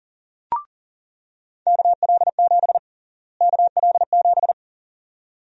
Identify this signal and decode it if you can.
Morse code, heard as a keyed tone, about 40 wpm, 700 Hz. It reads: KL7 KL7
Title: USA Call Sign Prefixes - Repeated 2x - 40wpm